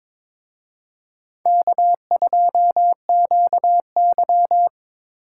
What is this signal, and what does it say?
Morse code, heard as a keyed tone, about 27 wpm, 700 Hz. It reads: K2QY
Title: Speed Racing - USA Call Signs 22wpm